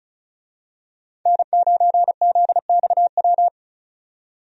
Morse code, heard as a keyed tone, about 36 wpm, 700 Hz. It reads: N9ZXW
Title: USA Call Signs - 5 Characters - 35wpm